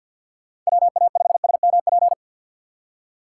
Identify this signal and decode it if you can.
Morse code, heard as a keyed tone, about 47 wpm, 700 Hz. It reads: WA5SMP
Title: USA Call Signs - 6 Characters - 4x Character Spacing - 50wpm